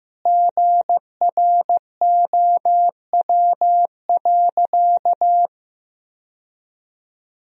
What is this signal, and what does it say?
Morse code, heard as a keyed tone, about 15 wpm, 700 Hz. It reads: GROW.
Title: Sentences from Top 1000 Words - Review Part 2 - 15wpm